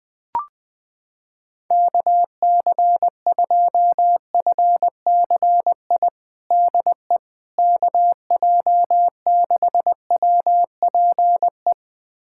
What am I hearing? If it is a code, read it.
Morse code, heard as a keyed tone, about 20 wpm, 700 Hz. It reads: KC2FCI DE KJ6WPE
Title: USA Callsign DE Callsign 20wpm